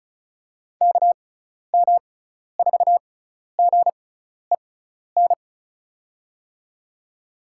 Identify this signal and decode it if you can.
Morse code, heard as a keyed tone, about 36 wpm, spaced as for 12 wpm, 690 Hz. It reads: KM4GEN
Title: USA Call Signs - 6 Characters - 6x Character Spacing - 35wpm